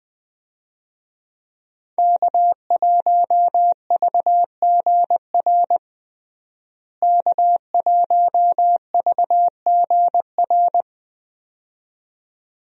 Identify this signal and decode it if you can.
Morse code, heard as a keyed tone, about 20 wpm, 700 Hz. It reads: K1VGR K1VGR
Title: USA Call Signs - 5 Characters - Repeated 2x - 20wpm